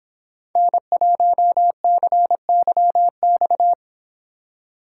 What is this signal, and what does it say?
Morse code, heard as a keyed tone, about 25 wpm, 700 Hz. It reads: N1CYX